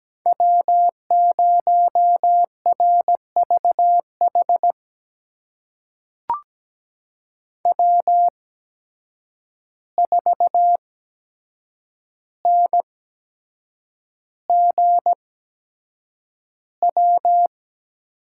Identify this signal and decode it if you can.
Morse code, heard as a keyed tone, about 17 wpm, spaced as for 7 wpm, 700 Hz. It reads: W0RVH W4NGW
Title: USA Call Signs - 5 Characters - 8x Character Spacing - 17wpm